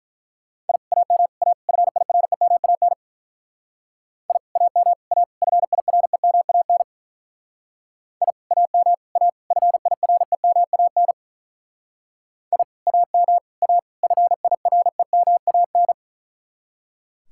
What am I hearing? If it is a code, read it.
Morse code, heard as a keyed tone, about 42 wpm, 700 Hz. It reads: I AM A FIREMAN I AM A FIREMAN I AM A FIREMAN I AM A FIREMAN
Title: Speed-Racing Common QSO Phrases - 35wpm